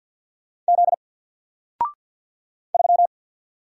Morse code, heard as a keyed tone, about 50 wpm, 700 Hz. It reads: Z 3
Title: Single Letter-Number 50wpm